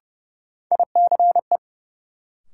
Morse code, heard as a keyed tone, about 32 wpm, 700 Hz. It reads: ICE
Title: Sentences from Top 1000 Words - New Word Drill - 30wpm